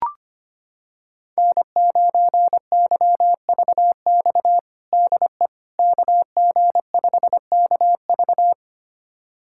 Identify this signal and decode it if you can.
Morse code, heard as a keyed tone, about 25 wpm, 700 Hz. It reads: N9YVX DE KG5KV
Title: USA Callsign DE Callsign - 5 Characters - 25wpm